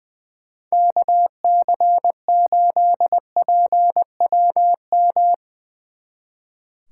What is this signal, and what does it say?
Morse code, heard as a keyed tone, about 23 wpm, 700 Hz. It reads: KC8PWM
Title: Speed Racing - USA 2x3 Call Signs 20wpm